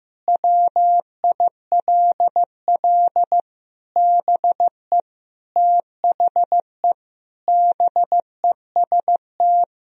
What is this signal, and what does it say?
Morse code, heard as a keyed tone, about 15 wpm, 700 Hz. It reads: WILL BE THE BEST
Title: Sentences from Top 1000 Words - Review Part 2 - 15wpm